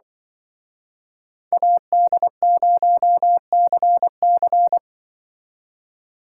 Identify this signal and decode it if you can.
Morse code, heard as a keyed tone, about 23 wpm, 700 Hz. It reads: AD0CC
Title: Speed Racing - USA Call Signs 20wpm